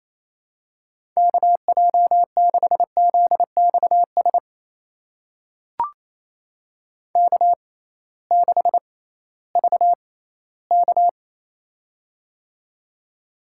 Morse code, heard as a keyed tone, about 28 wpm, spaced as for 16 wpm, 700 Hz. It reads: KJ6ZXS K6VK
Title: USA Call Signs - 6x Character Spacing - 28wpm